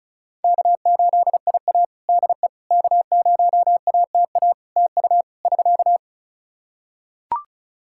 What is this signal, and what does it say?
Morse code, heard as a keyed tone, about 34 wpm, 700 Hz. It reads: K8IA DE K0ATA TU <SK>
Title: Practice QSO - Exchange 4 - Ending - 35wpm